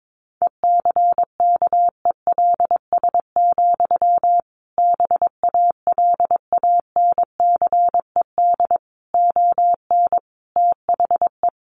Morse code, heard as a keyed tone, about 22 wpm, 700 Hz. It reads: ECKELS, BALANCED ON THE
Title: A Sound Of Thunder 22wpm